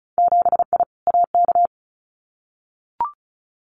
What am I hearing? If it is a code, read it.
Morse code, heard as a keyed tone, about 36 wpm, 700 Hz. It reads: 7I AK
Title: Winter Field Day Exchanges Encore 35wpm